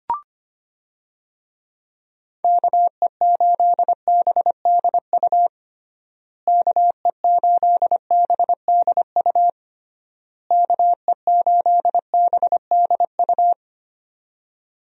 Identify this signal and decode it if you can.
Morse code, heard as a keyed tone, about 25 wpm, 700 Hz. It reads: KE8BDU KE8BDU KE8BDU
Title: USA Call Signs - 6 Characters - Repeated 3x - 25wpm